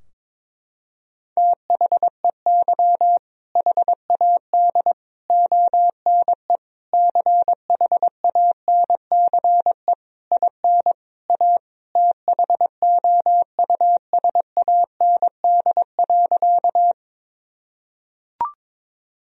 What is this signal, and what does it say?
Morse code, heard as a keyed tone, about 22 wpm, 700 Hz. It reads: THEY HAD ONE CHANCE IN A THOUSAND.
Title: Sentences from Top 1000 Words - Review Part 3 - 22wpm